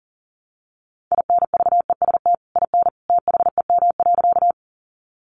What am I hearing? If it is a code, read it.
Morse code, heard as a keyed tone, about 40 wpm, 700 Hz. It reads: INVEST IN THEM.
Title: Sentences from Top 1100 Words 40wpm